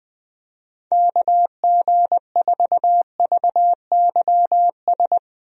Morse code, heard as a keyed tone, about 20 wpm, 700 Hz. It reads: KG4VYS
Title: USA Call Signs - 6 Characters - Repeated 2x - 20wpm